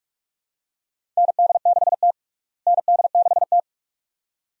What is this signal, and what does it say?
Morse code, heard as a keyed tone, about 46 wpm, 700 Hz. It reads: ND6T ND6T
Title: USA Call Signs - 4 Characters - Repeated 2x - 45wpm